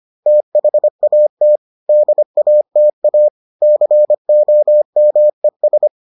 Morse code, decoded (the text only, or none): THAT DATA COMES